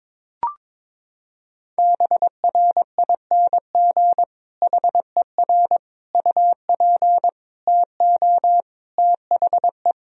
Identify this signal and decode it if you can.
Morse code, heard as a keyed tone, about 21 wpm, 700 Hz. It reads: BRING HER UP TO THE
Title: Sentences from Top 1000 Words - Review Part 3 - 22wpm